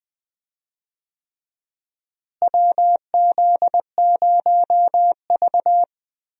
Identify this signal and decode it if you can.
Morse code, heard as a keyed tone, about 20 wpm, 700 Hz. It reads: WZ0V
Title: USA Call Signs - 4 Characters - Repeated 3x - 20wpm